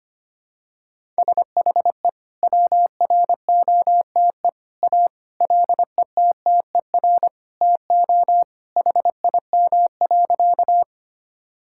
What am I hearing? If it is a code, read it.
Morse code, heard as a keyed tone, about 25 wpm, 700 Hz. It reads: SHE WROTE A LETTER TO HIM.
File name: practice / 25wpm / Sentences with 2-Word Phrases - 25wpm